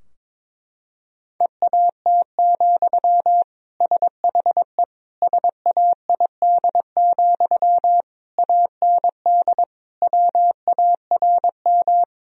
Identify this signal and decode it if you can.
Morse code, heard as a keyed tone, about 22 wpm, 700 Hz. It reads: EAT, SHE SAID, AND WARM
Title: Sentences from Top 1000 Words - Review Part 2 - 22wpm